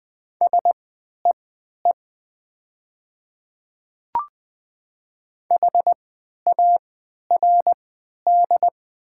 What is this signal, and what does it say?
Morse code, heard as a keyed tone, about 20 wpm, spaced as for 11 wpm, 700 Hz. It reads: SEE HARD
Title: 3x Character Spacing - Top 300 Words - 20wpm